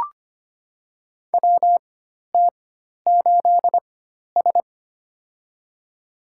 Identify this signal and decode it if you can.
Morse code, heard as a keyed tone, about 25 wpm, spaced as for 12 wpm, 700 Hz. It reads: WT8S
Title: USA Call Signs - 4 Characters - 4x Character Spacing - 25wpm